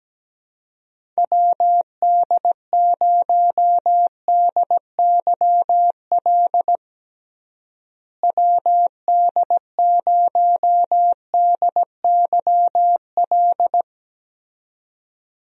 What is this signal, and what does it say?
Morse code, heard as a keyed tone, about 17 wpm, 700 Hz. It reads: WD0DYL WD0DYL
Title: USA Call Signs - 6 Characters - Repeated 2x - 17wpm